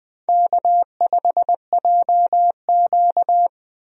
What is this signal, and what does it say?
Morse code, heard as a keyed tone, about 20 wpm, 700 Hz. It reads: K5JQ
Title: USA Call Signs - 4 Characters - 20wpm